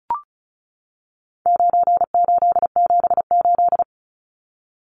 Morse code, heard as a keyed tone, about 34 wpm, 700 Hz. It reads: 9878
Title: Numbers - 4-Digit 35wpm